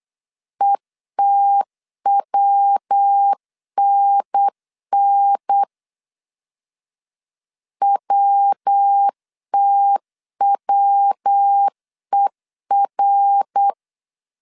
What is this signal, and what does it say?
Morse code, heard as a keyed tone, about 8 wpm, 800 Hz. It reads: ETWNN WTWER
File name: CW / DET / 1 3 1500C S B 8wpm 4141